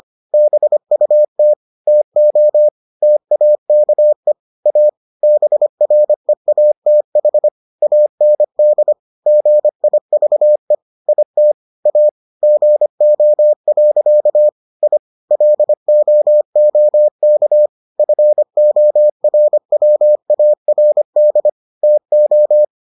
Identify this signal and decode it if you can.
Morse code, heard as a keyed tone, about 25 wpm, 600 Hz. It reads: BUT TO TAKE A BREATH AND GIVE IT A GO. I LOOK FORWARD TO